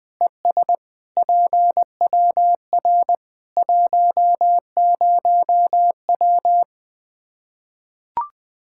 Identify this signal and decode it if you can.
Morse code, heard as a keyed tone, about 20 wpm, 700 Hz. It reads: ES PWR 10W
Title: K1USN SST QSO Practice 20wpm